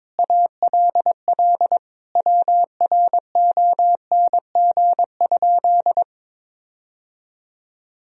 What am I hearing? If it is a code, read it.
Morse code, heard as a keyed tone, about 22 wpm, 700 Hz. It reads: ALL WRONG?
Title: Sentences from Top 1000 Words - Review Part 1 - 22wpm